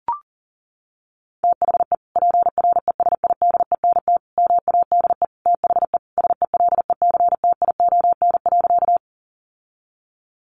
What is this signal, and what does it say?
Morse code, heard as a keyed tone, about 40 wpm, 700 Hz. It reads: THE PRESIDENT MADE THE SELECTION.
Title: Sentences from Top 1400 Words 40wpm